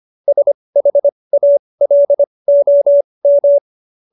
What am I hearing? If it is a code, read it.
Morse code, heard as a keyed tone, about 25 wpm, spaced as for 20 wpm, 590 Hz. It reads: SHALOM
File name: QSO310 20